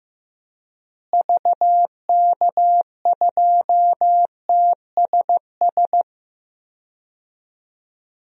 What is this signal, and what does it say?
Morse code, heard as a keyed tone, about 15 wpm, 700 Hz. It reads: VK2TSS
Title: DXCC-100 Call Signs - 6 Characters - Repeated 2x - 15wpm